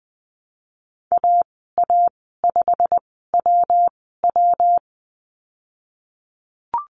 Morse code, heard as a keyed tone, about 19 wpm, spaced as for 14 wpm, 700 Hz. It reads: AA5WW
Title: 161 Lesson 26g - Letter g - Call Signs 20wpm